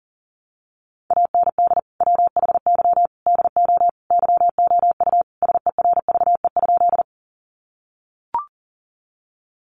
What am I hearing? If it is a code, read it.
Morse code, heard as a keyed tone, about 38 wpm, 700 Hz. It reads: AND WHY DO YOU SERVE?
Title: Sentences from Top 400 Words 40wpm